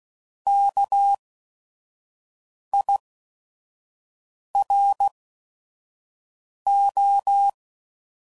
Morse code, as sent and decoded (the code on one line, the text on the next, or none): -.- .. .-. ---
KIRO